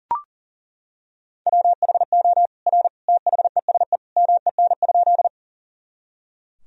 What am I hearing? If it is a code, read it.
Morse code, heard as a keyed tone, about 40 wpm, 700 Hz. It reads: WHO R THESE MEN?
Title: Sentences from Top 200 Words and Abbreviations 40wpm